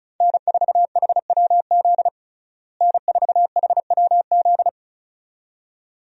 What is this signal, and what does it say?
Morse code, heard as a keyed tone, about 34 wpm, 700 Hz. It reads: N4HWZ N4HWZ
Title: USA Call Signs - 5 Characters - Repeated 2x - 35wpm